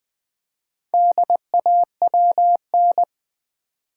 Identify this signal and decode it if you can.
Morse code, heard as a keyed tone, about 20 wpm, 690 Hz. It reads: DAWN
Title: Build A Word - 2 Letters At A Time - 4-Letter Words - 20wpm